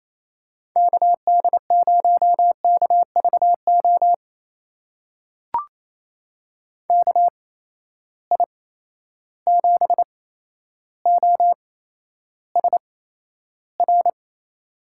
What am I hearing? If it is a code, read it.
Morse code, heard as a keyed tone, about 27 wpm, spaced as for 12 wpm, 700 Hz. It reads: KD0KVO KI7OSR
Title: USA Call Signs - 6 Characters - 8x Character Spacing - 28wpm